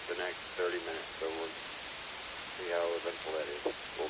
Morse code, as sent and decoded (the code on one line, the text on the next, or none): none